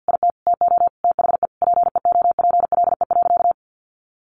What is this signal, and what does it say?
Morse code, heard as a keyed tone, about 50 wpm, 700 Hz. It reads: IT TO THE PEOPLE.
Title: Sentences from Top 1500 Words 50wpm